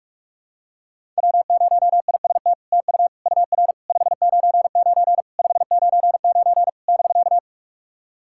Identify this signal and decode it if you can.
Morse code, heard as a keyed tone, about 44 wpm, 700 Hz. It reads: W0IST TU UR 599 599 <BK>
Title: SOTA QSO Practice 45wpm